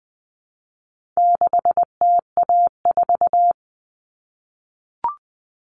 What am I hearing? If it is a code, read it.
Morse code, heard as a keyed tone, about 20 wpm, 700 Hz. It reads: 6TA4
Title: All Cut Numbers - 4-Digit 20wpm